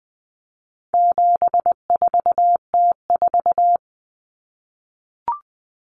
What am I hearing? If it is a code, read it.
Morse code, heard as a keyed tone, about 20 wpm, 700 Hz. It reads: 74T4
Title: Common Cut Numbers - 4-Digit 20wpm